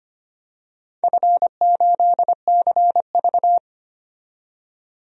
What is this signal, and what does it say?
Morse code, heard as a keyed tone, about 25 wpm, 700 Hz. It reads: F8CV
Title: DXCC-100 Call Signs - 4 Characters - 25wpm